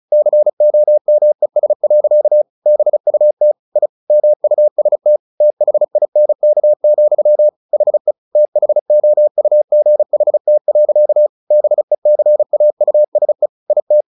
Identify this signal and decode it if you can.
Morse code, heard as a keyed tone, about 34 wpm, 600 Hz. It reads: COMES. BUT I MUST THINK, HE THOUGHT. BECAUSE IT
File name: the old man and the sea 2 35wpm